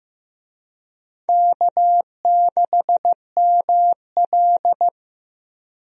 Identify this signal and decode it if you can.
Morse code, heard as a keyed tone, about 15 wpm, 700 Hz. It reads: K6ML
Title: USA Call Signs - 4 Characters - Repeated 2x - 15wpm